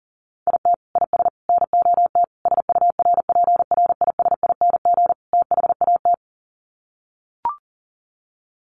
Morse code, heard as a keyed tone, about 40 wpm, 700 Hz. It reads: IT IS NOT SURPRISING THAT
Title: Sets of 5 Words 40wpm